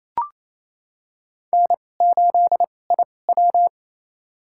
Morse code, heard as a keyed tone, about 27 wpm, spaced as for 20 wpm, 690 Hz. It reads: N8IW
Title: USA Call Signs - 4 Characters - 2x Character Spacing - 28wpm